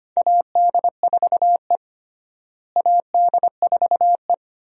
Morse code, heard as a keyed tone, about 25 wpm, 700 Hz. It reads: AD4E AD4E
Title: USA Call Signs - 4 Characters - Repeated 2x - 25wpm